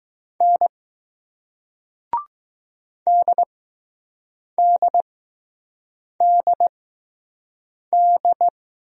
Morse code, heard as a keyed tone, about 20 wpm, spaced as for 6 wpm, 700 Hz. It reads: N DDDD